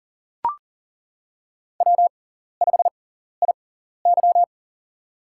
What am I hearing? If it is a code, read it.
Morse code, heard as a keyed tone, about 38 wpm, spaced as for 14 wpm, 700 Hz. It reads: W5IY
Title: USA Call Signs - 4 Characters - 6x Character Spacing - 40wpm